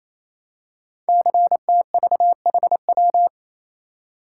Morse code, heard as a keyed tone, about 27 wpm, 700 Hz. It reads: CTVHW
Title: Code Groups - 5 Letters - 28wpm